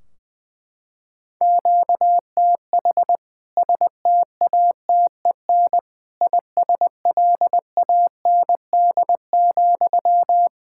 Morse code, heard as a keyed tone, about 20 wpm, 700 Hz. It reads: QTH STATEN ISLAND,
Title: K1USN SST QSO Practice 20wpm